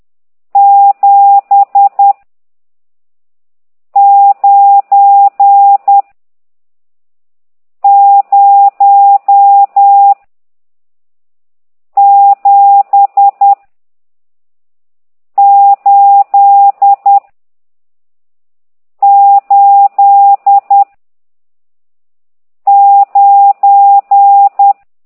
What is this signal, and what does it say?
Morse code, heard as a keyed tone, about 10 wpm, 800 Hz. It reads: 7 9 0 7 8 8 9
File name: ct2idw / exerc14